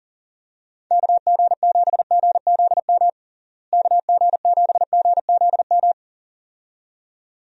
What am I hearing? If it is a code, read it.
Morse code, heard as a keyed tone, about 37 wpm, 700 Hz. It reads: KG7GZM KG7GZM
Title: USA Call Signs - 6 Characters - Repeated 2x - 40wpm